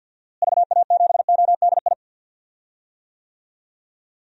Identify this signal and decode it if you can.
Morse code, heard as a keyed tone, about 52 wpm, 700 Hz. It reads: VA7ODI
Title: DXCC-100 Call Signs - 6 Characters - Repeated 2x - 50wpm